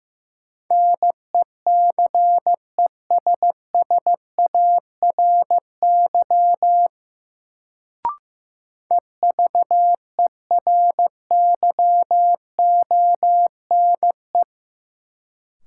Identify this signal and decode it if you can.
Morse code, heard as a keyed tone, about 15 wpm, 700 Hz. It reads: NECESSARY EVERYONE
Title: Sentences from Top 800 Words - New Word Drill - 15wpm